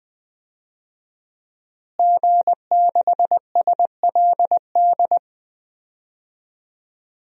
Morse code, heard as a keyed tone, about 20 wpm, 700 Hz. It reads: G6SLD
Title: DXCC-100 Call Signs - 5 Characters - Repeated 3x - 20wpm